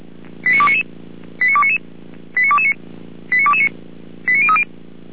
Five-tone selective call (ZVEI), heard as quick runs of five tones, tones of about 70 ms. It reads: 89299 89299 89229 89229 89030